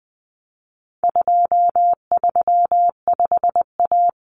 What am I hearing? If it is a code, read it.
Morse code, heard as a keyed tone, about 20 wpm, 700 Hz. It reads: 235A
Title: Common Cut Numbers - 4-Digit 20wpm